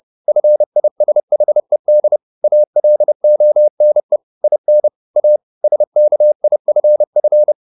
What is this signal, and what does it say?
Morse code, heard as a keyed tone, about 30 wpm, 600 Hz. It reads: FISHED ALONE IN A SKIFF
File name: the old man and the sea / the old man and the sea 1 30wpm